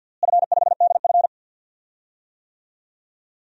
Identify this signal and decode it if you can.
Morse code, heard as a keyed tone, about 50 wpm, 700 Hz. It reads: F5DF
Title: DXCC-100 Call Signs - 4 Characters - Repeated 3x - 50wpm